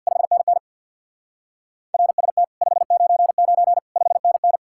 Morse code, heard as a keyed tone, about 64 wpm, 700 Hz. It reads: 5NN RST 599 5NN